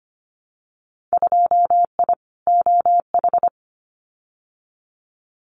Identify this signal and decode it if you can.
Morse code, heard as a keyed tone, about 25 wpm, 700 Hz. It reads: 2I OH
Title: Winter Field Day Exchanges 25wpm